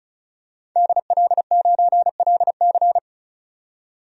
Morse code, heard as a keyed tone, about 36 wpm, 700 Hz. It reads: DL9LC
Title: DXCC-100 Call Signs - 5 Characters - 35wpm